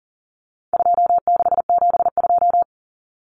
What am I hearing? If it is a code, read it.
Morse code, heard as a keyed tone, about 40 wpm, 700 Hz. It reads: 2672
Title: Numbers - 4-Digit 40wpm